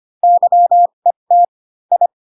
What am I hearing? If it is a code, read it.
Morse code, heard as a keyed tone, about 25 wpm, spaced as for 22 wpm, 700 Hz. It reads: YET I
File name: Pinocchio / Pinocchio ch2 18 25 ik1hge